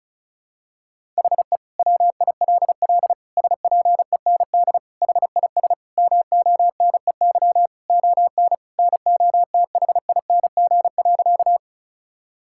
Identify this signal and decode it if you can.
Morse code, heard as a keyed tone, about 34 wpm, 700 Hz. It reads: HE WILL SPEND HIS MONEY ON NOTHING.
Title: Sentences from Top 1000 Words - Review Part 5 - 35wpm